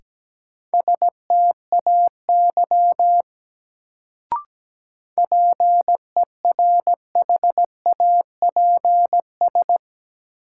Morse code, heard as a keyed tone, about 17 wpm, 700 Hz. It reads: STAY PERHAPS
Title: Sentences from Top 500 Words - New Word Drill - 17wpm